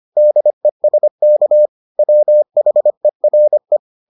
Morse code, heard as a keyed tone, about 25 wpm, 600 Hz. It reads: DESK WHERE